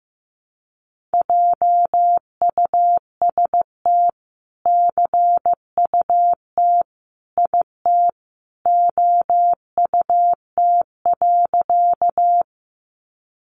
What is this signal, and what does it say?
Morse code, heard as a keyed tone, about 15 wpm, 700 Hz. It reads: JUST CUT IT OUT.
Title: Sentences from Top 300 Words 15wpm